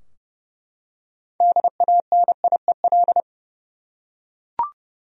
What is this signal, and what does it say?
Morse code, heard as a keyed tone, about 30 wpm, 700 Hz. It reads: DANIEL
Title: USA Common Names - 30wpm